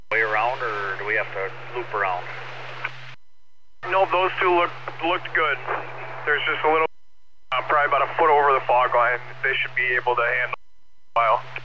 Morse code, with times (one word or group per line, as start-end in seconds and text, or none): none